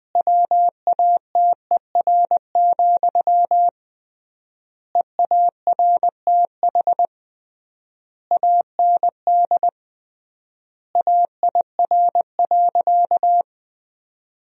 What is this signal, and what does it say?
Morse code, heard as a keyed tone, about 20 wpm, 700 Hz. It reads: WATER, EARTH AND AIR.
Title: 3x Word-Spacing - Sentences from Top 400 Words - 20wpm